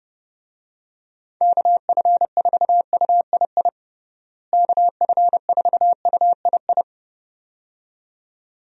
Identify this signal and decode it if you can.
Morse code, heard as a keyed tone, about 31 wpm, 700 Hz. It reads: KF4UII KF4UII
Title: USA Call Signs - 6 Characters - Repeated 2x - 30wpm